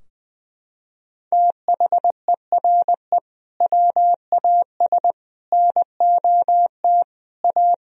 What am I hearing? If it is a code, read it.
Morse code, heard as a keyed tone, about 20 wpm, 700 Hz. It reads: THERE WAS NOT A
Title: Sentences from Top 1000 Words - Review Part 4 - 20wpm